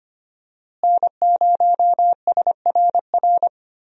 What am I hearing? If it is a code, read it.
Morse code, heard as a keyed tone, about 25 wpm, 700 Hz. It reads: N0SRR
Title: USA Call Signs - 5 Characters - Repeated 2x - 25wpm